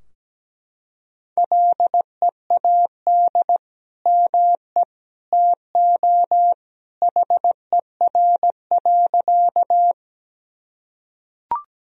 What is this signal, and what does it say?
Morse code, heard as a keyed tone, about 17 wpm, 700 Hz. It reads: LEAD ME TO HER.
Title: Sentences from Top 1000 Words - Review Part 2 - 17wpm